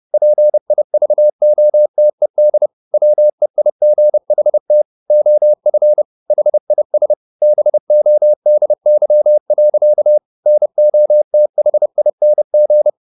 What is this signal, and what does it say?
Morse code, heard as a keyed tone, about 30 wpm, 600 Hz. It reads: PIVOTED WEIGHT OF HIS BODY. NOTHING
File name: the old man and the sea / the old man and the sea 1 30wpm